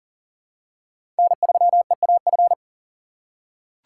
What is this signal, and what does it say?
Morse code, heard as a keyed tone, about 38 wpm, 690 Hz. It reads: N3EAF
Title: USA Call Signs - 5 Characters - 3x Character Spacing - 40wpm